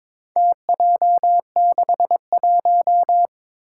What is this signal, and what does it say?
Morse code, heard as a keyed tone, about 22 wpm, 700 Hz. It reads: TJ61
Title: Four Challenging Letter-Number Combinations 22wpm